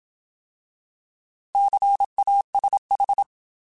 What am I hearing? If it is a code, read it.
Morse code, heard as a keyed tone, about 27 wpm, 800 Hz. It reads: CASH
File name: Alphabetical / Alphabetical mp3s / Cs @25